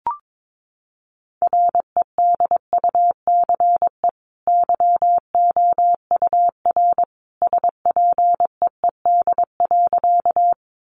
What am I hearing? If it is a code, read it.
Morse code, heard as a keyed tone, about 22 wpm, 700 Hz. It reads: REDUCE YOUR SPEED.
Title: Sentences from Top 1100 Words 22wpm